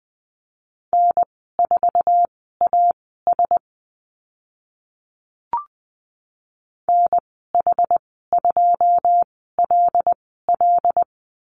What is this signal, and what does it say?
Morse code, heard as a keyed tone, about 20 wpm, spaced as for 14 wpm, 700 Hz. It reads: N4AS NH2LL
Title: 082 Lesson 14g - Number 2 - Call Signs 20wpm